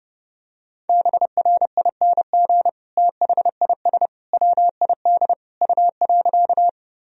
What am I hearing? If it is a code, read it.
Morse code, heard as a keyed tone, about 32 wpm, 700 Hz. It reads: BRING THIS WID U.
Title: Sentences from Top 500 Words and Abbreviations 30wpm